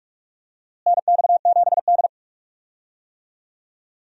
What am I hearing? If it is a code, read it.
Morse code, heard as a keyed tone, about 46 wpm, 700 Hz. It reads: NX7D
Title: USA Call Signs - 4 Characters - Repeated 2x - 45wpm